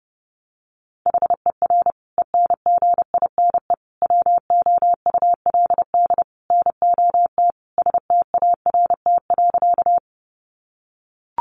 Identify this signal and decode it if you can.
Morse code, heard as a keyed tone, about 30 wpm, 700 Hz. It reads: HER ENGINE WOULD NOT START.
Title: Sentences from Top 1500 Words 30wpm